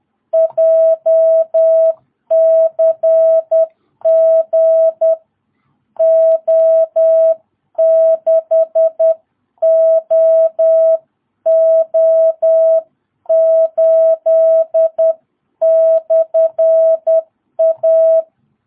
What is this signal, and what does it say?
Morse code, heard as a keyed tone, about 10 wpm, 660 Hz. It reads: JCG O6OO8/A
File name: jh8cbh / 2022cw